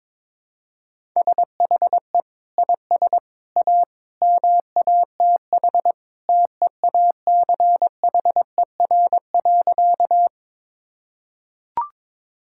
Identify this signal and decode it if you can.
Morse code, heard as a keyed tone, about 22 wpm, 700 Hz. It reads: SHE IS A MATH TEACHER.
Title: Sentences from Top 1000 Words - Review Part 6 - 22wpm